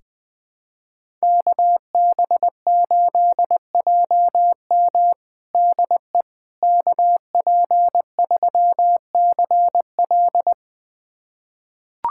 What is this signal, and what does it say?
Morse code, heard as a keyed tone, about 20 wpm, 700 Hz. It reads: KB8JM DE KP3CL
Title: USA Callsign DE Callsign - 5 Characters - 20wpm